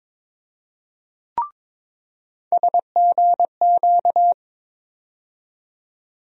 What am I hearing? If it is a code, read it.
Morse code, heard as a keyed tone, about 23 wpm, 700 Hz. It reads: SGQ
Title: USA Call Sign Postfixes - 3 Letters 22wpm